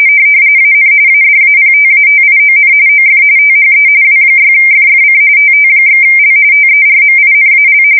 RTTY, about 45.45 baud. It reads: AN ADAPTATION OF THE W6FFC TT/L TERMINAL
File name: fsk170 / teletype-RTTY